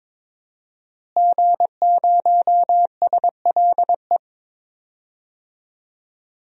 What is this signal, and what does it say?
Morse code, heard as a keyed tone, about 21 wpm, 700 Hz. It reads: G0SLE
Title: DXCC-100 Call Signs - 5 Characters - Repeated 2x - 22wpm